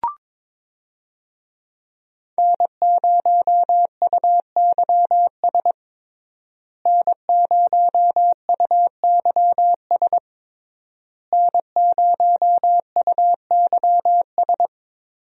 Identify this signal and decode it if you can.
Morse code, heard as a keyed tone, about 22 wpm, 700 Hz. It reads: N0UYS N0UYS N0UYS
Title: USA Call Signs - 5 Characters - Repeated 3x - 22wpm